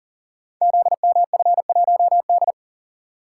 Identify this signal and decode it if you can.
Morse code, heard as a keyed tone, about 40 wpm, 700 Hz. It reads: ZMF1D
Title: Code Groups - 5 Letter:Numbers - 40wpm